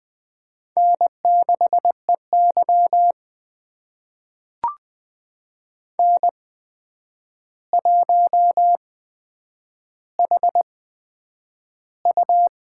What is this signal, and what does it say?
Morse code, heard as a keyed tone, about 20 wpm, spaced as for 9 wpm, 700 Hz. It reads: N6EY N1HU
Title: USA Call Signs - 4 Characters - 8x Character Spacing - 20wpm